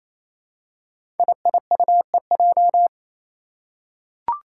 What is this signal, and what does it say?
Morse code, heard as a keyed tone, about 28 wpm, 700 Hz. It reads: IIUEJ
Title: Five Challenging Letter-Number Combinations 28wpm